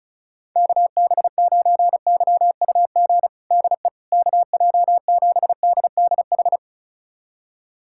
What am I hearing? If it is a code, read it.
Morse code, heard as a keyed tone, about 36 wpm, 700 Hz. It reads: KB9YUG DE KJ7DDH
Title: USA Callsign DE Callsign - 6 Characters - 35wpm